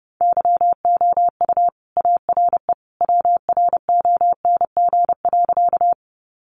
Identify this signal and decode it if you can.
Morse code, heard as a keyed tone, about 30 wpm, 700 Hz. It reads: YOU ARE WRONG.
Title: Sentences from Top 900 Words 30wpm